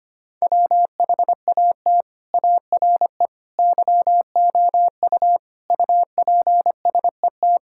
Morse code, heard as a keyed tone, about 25 wpm, 700 Hz. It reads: WHAT ARE YOU UPSET
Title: Sentences from Top 1000 Words - Review Part 5 - 25wpm